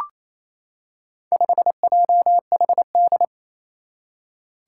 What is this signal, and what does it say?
Morse code, heard as a keyed tone, about 28 wpm, 700 Hz. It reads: 5JHD
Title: Four Challenging Letter-Number Combinations 28wpm